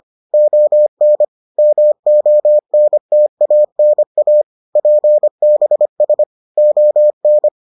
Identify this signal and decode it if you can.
Morse code, heard as a keyed tone, about 25 wpm, 600 Hz. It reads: ON MONTANA PBS ON